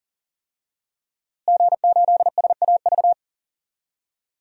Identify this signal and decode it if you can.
Morse code, heard as a keyed tone, about 40 wpm, 700 Hz. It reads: G8SAV
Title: DXCC-100 Call Signs - 5 Characters - Repeated 2x - 40wpm